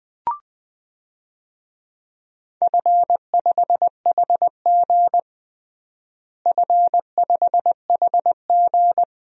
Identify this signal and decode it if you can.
Morse code, heard as a keyed tone, about 20 wpm, 700 Hz. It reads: F5HG F5HG
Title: DXCC-100 Call Signs - 4 Characters - Repeated 3x - 20wpm